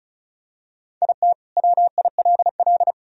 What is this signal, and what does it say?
Morse code, heard as a keyed tone, about 34 wpm, 700 Hz. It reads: IT WILL
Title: Sentences from Top 1000 Words - Review Part 4 - 35wpm